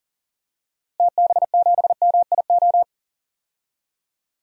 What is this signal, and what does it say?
Morse code, heard as a keyed tone, about 40 wpm, 700 Hz. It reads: TB7MIO
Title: DXCC-100 Call Signs - 6 Characters - Repeated 2x - 40wpm